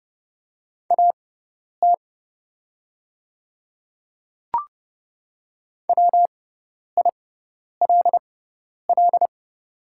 Morse code, heard as a keyed tone, about 30 wpm, spaced as for 10 wpm, 700 Hz. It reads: AT WILL
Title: 6x Character Spacing - Top 200 Words - 30wpm